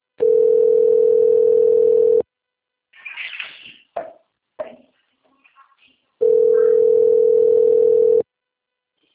Ringback tone (North America), 440 + 480 Hz.
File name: rg-1600-unknown- - - .19173